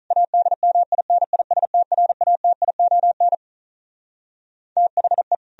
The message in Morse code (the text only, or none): ADMINISTRATION THE